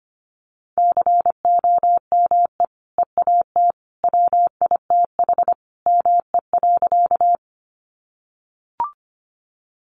Morse code, heard as a keyed tone, about 25 wpm, 700 Hz. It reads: COME EAT WITH ME.
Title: Sentences from Top 300 Words 25wpm